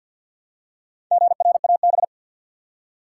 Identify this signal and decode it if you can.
Morse code, heard as a keyed tone, about 52 wpm, 700 Hz. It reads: GRAB
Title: Four Letter Words - 50wpm